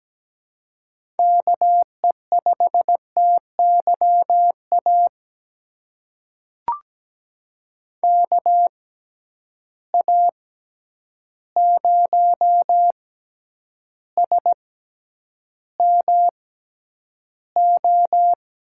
Morse code, heard as a keyed tone, about 17 wpm, spaced as for 9 wpm, 700 Hz. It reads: KE5TYA KA0SMO
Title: USA Call Signs - 6 Characters - 6x Character Spacing - 17wpm